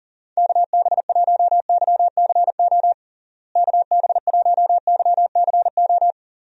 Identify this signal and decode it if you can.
Morse code, heard as a keyed tone, about 40 wpm, 700 Hz. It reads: KB1YCO KB1YCO
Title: USA Call Signs - 6 Characters - Repeated 2x - 40wpm